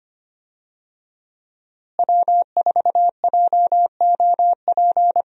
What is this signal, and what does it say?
Morse code, heard as a keyed tone, about 25 wpm, 700 Hz. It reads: W4JOP
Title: USA Call Signs - 5 Characters - Repeated 3x - 25wpm